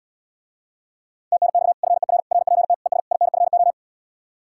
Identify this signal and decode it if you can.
Morse code, heard as a keyed tone, about 79 wpm, 700 Hz. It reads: TNX FER RPT ES INFO